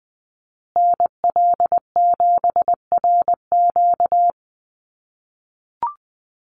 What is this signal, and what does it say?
Morse code, heard as a keyed tone, about 20 wpm, 700 Hz. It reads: NL7RQ
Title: USA Call Signs Repeated 3x 20wpm